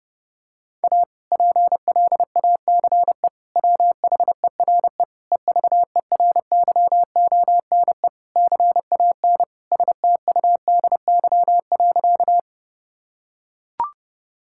Morse code, heard as a keyed tone, about 32 wpm, 700 Hz. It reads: A PLACE WHERE EVERYONE CAN STUDY.
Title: Sentences from Top 1000 Words - Review Part 5 - 30wpm